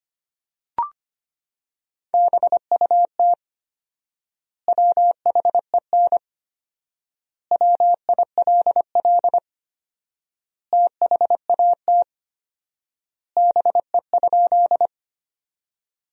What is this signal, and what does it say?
Morse code, heard as a keyed tone, about 25 wpm, 700 Hz. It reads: BUT WHEN WILL THAT BE?
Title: 4x Word-Spacing - Sentences from Top 100 Words - 25wpm